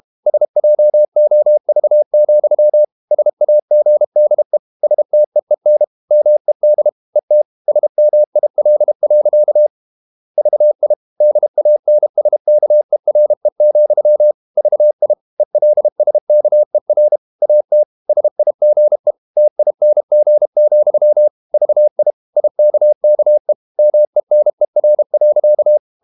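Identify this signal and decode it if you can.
Morse code, heard as a keyed tone, about 32 wpm, 600 Hz. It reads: SJOV, SAGDE STEEN MED ET SMIL. VI DANSKERE, VI ELSKER AT SIGE TING, VI IKKE MENER.
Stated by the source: High-speed QRQ CW at 32 WPM.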